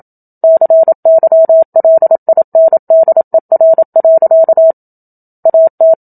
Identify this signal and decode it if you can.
Morse code, heard as a keyed tone, about 27 wpm, 650 Hz. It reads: CYLINDER. AT
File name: Book Two / 01-Chapter 1 - 27 WPM